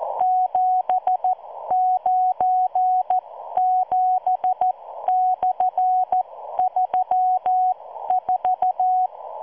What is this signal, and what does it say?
Morse code, heard as a keyed tone, about 14 wpm, spaced as for 12 wpm, 740 Hz. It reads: 797/34